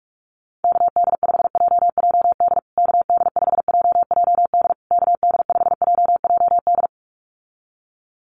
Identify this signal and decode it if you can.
Morse code, heard as a keyed tone, about 44 wpm, 700 Hz. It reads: KD5JJD KD5JJD KD5JJD
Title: USA Call Signs Repeated 3x 45wpm